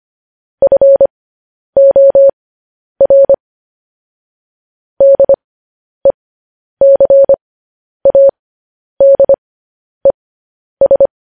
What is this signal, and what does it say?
Morse code, heard as a keyed tone, about 25 wpm, spaced as for 10 wpm, 550 Hz. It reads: FOR DECADES
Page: News Headlines in Morse Code at 10 WPM